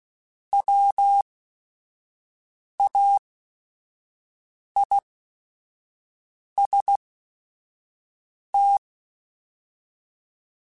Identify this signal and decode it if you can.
Morse code, heard as a keyed tone, about 16 wpm, spaced as for 5 wpm, 800 Hz. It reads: WAIST
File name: Chuckles / Chuckles-mp3s / MiddleAge @5